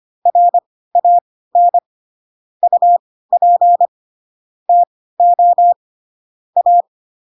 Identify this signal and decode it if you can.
Morse code, heard as a keyed tone, about 25 wpm, spaced as for 16 wpm, 700 Hz. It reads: RAN UP TO A
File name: Pinocchio / Pinocchio ch6 10 25 ik1hge